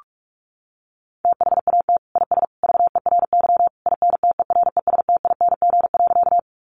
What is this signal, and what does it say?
Morse code, heard as a keyed tone, about 44 wpm, 700 Hz. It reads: THAT IS VERY INTERESTING.
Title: Sentences from Top 1000 Words 45wpm